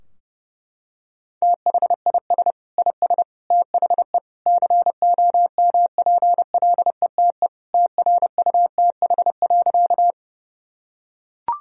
Morse code, heard as a keyed tone, about 32 wpm, 700 Hz. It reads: THIS IS THE COMPLETE TRUTH.
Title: Sentences from Top 1000 Words - Review Part 2 - 30wpm